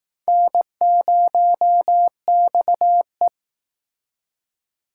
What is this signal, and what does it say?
Morse code, heard as a keyed tone, about 19 wpm, 700 Hz. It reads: N0XE